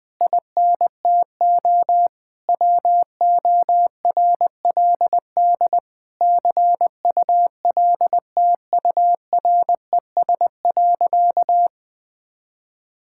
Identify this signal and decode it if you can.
Morse code, heard as a keyed tone, about 20 wpm, 700 Hz. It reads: INTO WORLD CULTURES.
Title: Difficult Sentences with Hints - 20wpm